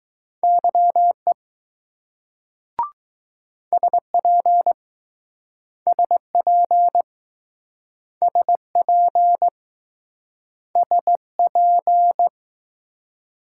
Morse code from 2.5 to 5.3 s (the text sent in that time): SP